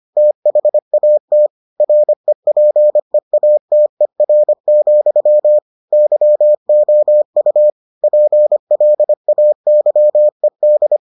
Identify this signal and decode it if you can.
Morse code, heard as a keyed tone, about 25 wpm, 600 Hz. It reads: THAT REPEATER, YOU PLAYED